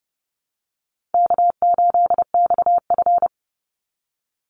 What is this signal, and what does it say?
Morse code, heard as a keyed tone, about 30 wpm, 700 Hz. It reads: K8XF
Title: Call Signs - 30wpm